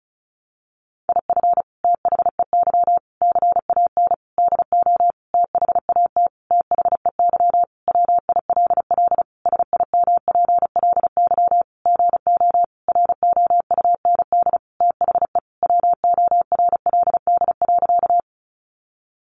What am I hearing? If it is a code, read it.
Morse code, heard as a keyed tone, about 34 wpm, 700 Hz. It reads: IF THEY CAN DO THAT THEY WILL SIMPLY GO ROUND THE WORLD.
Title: Sentences from Top 600 Words 35wpm